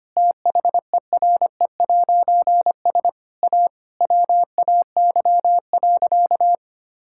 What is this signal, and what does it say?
Morse code, heard as a keyed tone, about 25 wpm, 700 Hz. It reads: THERE'S A WAY.
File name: practice / 25wpm / Difficult Sentences with Hints - 25wpm